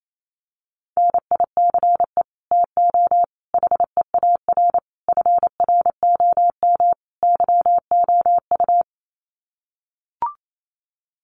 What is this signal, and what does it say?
Morse code, heard as a keyed tone, about 28 wpm, 700 Hz. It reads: NICE TO HEAR FROM YOU
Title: Common QSO Phrases 28wpm